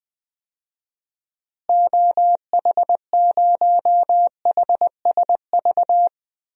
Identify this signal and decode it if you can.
Morse code, heard as a keyed tone, about 20 wpm, 700 Hz. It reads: OH0HSV
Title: DXCC-100 Call Signs - 6 Characters - Repeated 2x - 20wpm